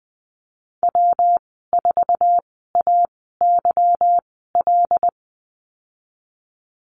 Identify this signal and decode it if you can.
Morse code, heard as a keyed tone, about 20 wpm, spaced as for 15 wpm, 700 Hz. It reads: W4AYL
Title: 147 Lesson 24g - Letter y - Call Signs 20wpm